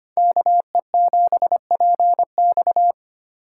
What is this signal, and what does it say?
Morse code, heard as a keyed tone, about 32 wpm, 700 Hz. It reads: KE7PX